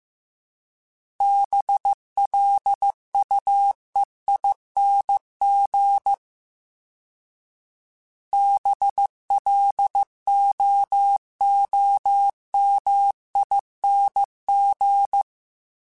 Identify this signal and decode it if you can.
Morse code, heard as a keyed tone, about 15 wpm, 800 Hz. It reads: BLUEING BLOOMING
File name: Special INGs-mp3s / Special INGs-mp3s / Sngl ing @13